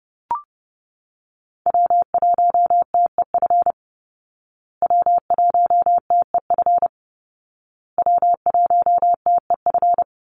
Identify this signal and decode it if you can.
Morse code, heard as a keyed tone, about 30 wpm, 700 Hz. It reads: W1TEF W1TEF W1TEF
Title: Call Signs Repeated 2x - 30wpm